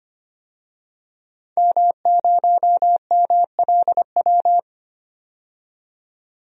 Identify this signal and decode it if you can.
Morse code, heard as a keyed tone, about 25 wpm, 690 Hz. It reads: M0MLW
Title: DXCC-100 Call Signs - 5 Characters - 25wpm